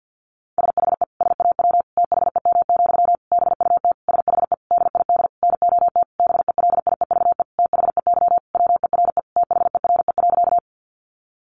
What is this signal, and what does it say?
Morse code, heard as a keyed tone, about 48 wpm, 700 Hz. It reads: SHE SAW THEM, BUT SHE DID NOT BELIEVE THEY WERE THERE.
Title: Sentences from Top 1000 Words 50wpm